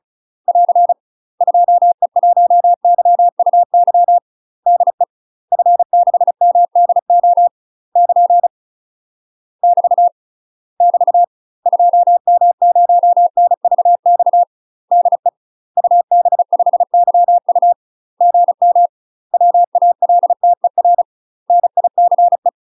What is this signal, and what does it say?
Morse code, heard as a keyed tone, about 36 wpm, 700 Hz. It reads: <AR> 2E1YUY DE F6MDO <KN> <BT> <BT> 2M0NVX DE UB5YU GM WALTER NICE
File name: UK MASTER QSOs TRAINING / 35WPM / UK MASTER QSOs 0008